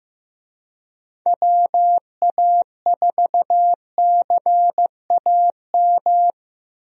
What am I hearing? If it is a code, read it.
Morse code, heard as a keyed tone, about 15 wpm, 700 Hz. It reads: WA4CAM
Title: USA Call Signs - 6 Characters - Repeated 2x - 15wpm